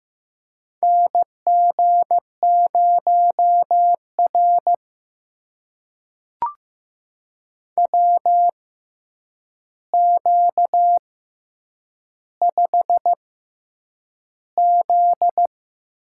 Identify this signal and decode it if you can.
Morse code, heard as a keyed tone, about 15 wpm, spaced as for 8 wpm, 700 Hz. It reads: NG0R WQ5Z
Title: USA Call Signs - 4 Characters - 6x Character Spacing - 15wpm